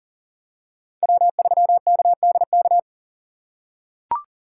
Morse code, heard as a keyed tone, about 40 wpm, 700 Hz. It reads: W3KDK
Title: USA Call Signs - 5 Characters - 3x Character Spacing - 40wpm